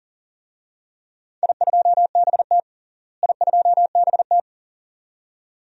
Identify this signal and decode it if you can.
Morse code, heard as a keyed tone, about 40 wpm, 700 Hz. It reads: I2BT I2BT
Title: DXCC-100 Call Signs - 4 Characters - Repeated 2x - 40wpm